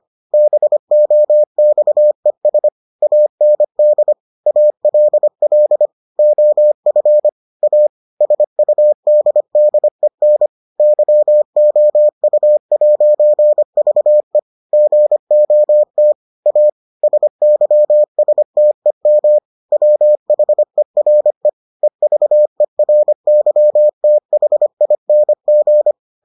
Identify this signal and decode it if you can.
Morse code, heard as a keyed tone, about 25 wpm, 600 Hz. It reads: BOXES AND ALL OF A SUDDEN YOU'VE GOT A SYSTEM WHERE EVERYTHING